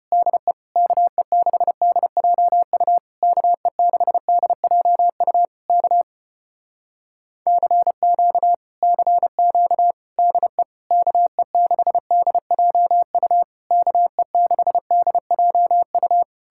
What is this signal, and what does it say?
Morse code, heard as a keyed tone, about 33 wpm, 700 Hz. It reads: DE KE6DJU KE6DJU K CQ CQ DE KE6DJU KE6DJU